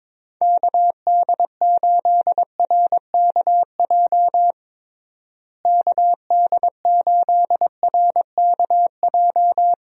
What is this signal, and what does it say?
Morse code, heard as a keyed tone, about 22 wpm, 700 Hz. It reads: KD8RKJ KD8RKJ
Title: USA Call Signs - 6 Characters - Repeated 3x - 22wpm